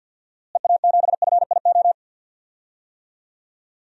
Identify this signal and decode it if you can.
Morse code, heard as a keyed tone, about 52 wpm, 700 Hz. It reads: EA7FIO
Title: DXCC-100 Call Signs - 6 Characters - Repeated 2x - 50wpm